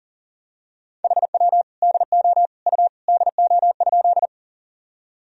Morse code, heard as a keyed tone, about 40 wpm, 700 Hz. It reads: HW DO U DO?
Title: Sentences from Top 100 Words and Abbreviations 40wpm